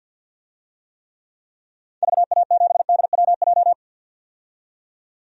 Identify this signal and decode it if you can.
Morse code, heard as a keyed tone, about 51 wpm, 700 Hz. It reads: VA7DWJ
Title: DXCC-100 Call Signs - 6 Characters - 50wpm